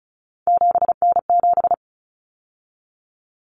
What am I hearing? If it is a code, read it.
Morse code, heard as a keyed tone, about 34 wpm, 700 Hz. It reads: 7N7
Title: Common Cut Numbers - 3-Digit 35wpm